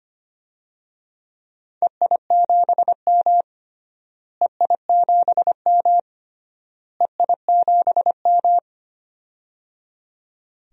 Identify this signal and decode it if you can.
Morse code, heard as a keyed tone, about 25 wpm, 700 Hz. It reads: EI7M EI7M EI7M
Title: DXCC-100 Call Signs - 4 Characters - Repeated 3x - 25wpm